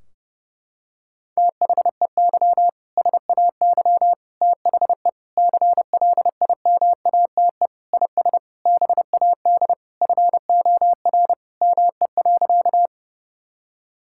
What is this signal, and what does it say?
Morse code, heard as a keyed tone, about 30 wpm, 700 Hz. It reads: THEY SAY THE CLIMATE IS BAD FOR ME.
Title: Sentences from Top 1600 Words 30wpm